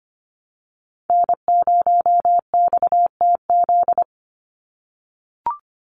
Morse code, heard as a keyed tone, about 25 wpm, 690 Hz. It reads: N0XTZ
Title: USA Call Signs 25wpm